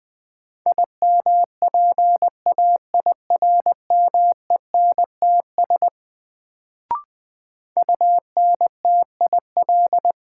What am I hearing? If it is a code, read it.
Morse code, heard as a keyed tone, about 20 wpm, 700 Hz. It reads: IMPAIRMENTS UNTIL
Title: Sets of 5 Words Encore - New Word Drill - 20wpm